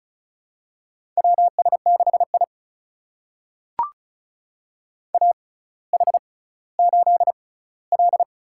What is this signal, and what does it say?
Morse code, heard as a keyed tone, about 36 wpm, spaced as for 18 wpm, 700 Hz. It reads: WS6I AH8L
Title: USA Call Signs - 4 Characters - 6x Character Spacing - 35wpm